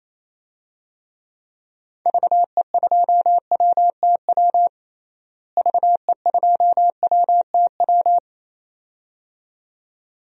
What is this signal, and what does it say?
Morse code, heard as a keyed tone, about 28 wpm, 700 Hz. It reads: VE2WTW VE2WTW
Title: DXCC-100 Call Signs - 6 Characters - Repeated 2x - 28wpm